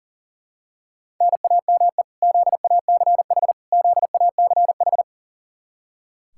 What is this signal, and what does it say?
Morse code, heard as a keyed tone, about 40 wpm, 700 Hz. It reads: NAME ZACH ZACH
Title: Practice QSO - Exchange 1 - Beginning - 40wpm